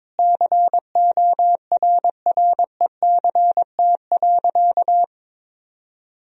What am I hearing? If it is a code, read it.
Morse code, heard as a keyed tone, about 22 wpm, 700 Hz. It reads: CORRECT.
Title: Sentences from Top 1000 Words - Review Part 3 - 22wpm